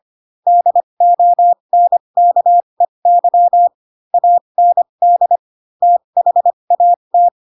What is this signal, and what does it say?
Morse code, heard as a keyed tone, about 25 wpm, spaced as for 22 wpm, 700 Hz. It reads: DONKEY AND THAT
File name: Pinocchio / Pinocchio ch4 18 25 ik1hge